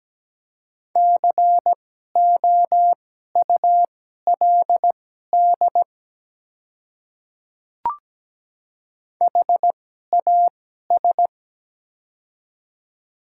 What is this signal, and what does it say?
Morse code, heard as a keyed tone, about 17 wpm, spaced as for 12 wpm, 700 Hz. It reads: COULD HAS
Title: 2x Character Spacing - Top 100 Words - 17wpm